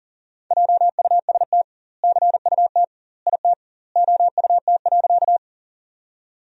Morse code, heard as a keyed tone, about 40 wpm, 700 Hz. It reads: JUST CUT IT OUT.
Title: 2x Word-Spacing - Sentences from Top 300 Words - 40wpm